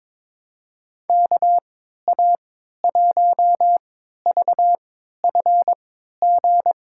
Morse code, heard as a keyed tone, about 22 wpm, spaced as for 13 wpm, 700 Hz. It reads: KA1VFG
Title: USA Call Signs - 6 Characters - 3x Character Spacing - 22wpm